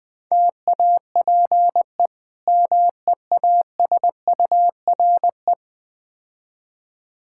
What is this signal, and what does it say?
Morse code, heard as a keyed tone, about 20 wpm, 700 Hz. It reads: TAPE MEASURE
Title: Difficult Words with Hints - 20wpm